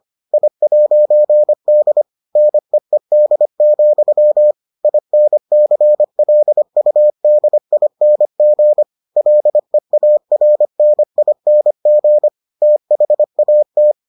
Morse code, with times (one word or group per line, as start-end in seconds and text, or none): 0.3-2.0 I'D
2.4-4.5 NEED,
4.8-8.8 INCLUDING
9.2-12.3 LEARNING
12.6-13.9 THAT